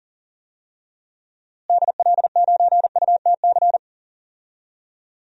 Morse code, heard as a keyed tone, about 37 wpm, 700 Hz. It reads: DL9UTC
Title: DXCC-100 Call Signs - 6 Characters - Repeated 3x - 40wpm